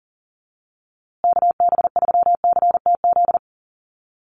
Morse code, heard as a keyed tone, about 40 wpm, 700 Hz. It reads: KB3CTZ
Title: USA Call Signs Repeated 3x 40wpm